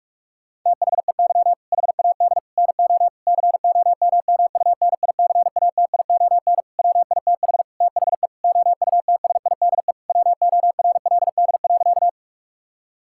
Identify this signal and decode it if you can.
Morse code, heard as a keyed tone, about 46 wpm, 700 Hz. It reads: THEY HAD NO COMMUNICATION WITH THE OUTSIDE WORLD.
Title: Sentences from Top 1000 Words - Review Part 6 - 45wpm